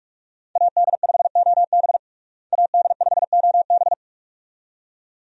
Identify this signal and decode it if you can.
Morse code, heard as a keyed tone, about 46 wpm, 700 Hz. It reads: AD5OB AD5OB
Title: USA Call Signs - 5 Characters - Repeated 2x - 45wpm